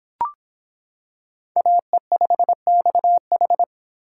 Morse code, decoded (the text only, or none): AE5XH